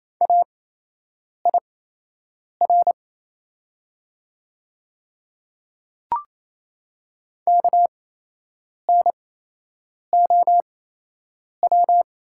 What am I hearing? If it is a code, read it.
Morse code, heard as a keyed tone, about 28 wpm, spaced as for 8 wpm, 700 Hz. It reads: AIR KNOW
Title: 8x Character Spacing - Top 200 Words - 28wpm